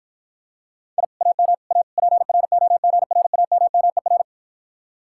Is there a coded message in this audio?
Yes: Morse code from 1.0 to 4.2 s, heard as a keyed tone, about 54 wpm, 700 Hz. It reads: I AM A PROGRAMMER